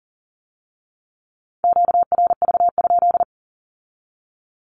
Morse code, heard as a keyed tone, about 42 wpm, 700 Hz. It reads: QRV?
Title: Q-Codes with Spoken Meaning - 40wpm